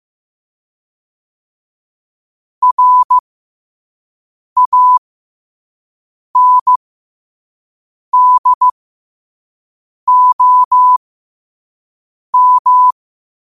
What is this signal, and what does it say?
Morse code, heard as a keyed tone, about 15 wpm, spaced as for 5 wpm, 1000 Hz. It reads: RANDOM